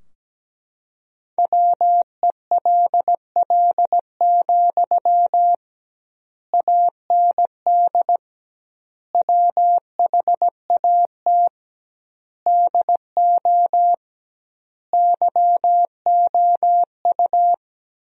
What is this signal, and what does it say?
Morse code, heard as a keyed tone, about 17 wpm, 700 Hz. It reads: WELL, AND WHAT DO YOU
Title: 2x Word-Spacing - Sentences from Top 200 Words - 17wpm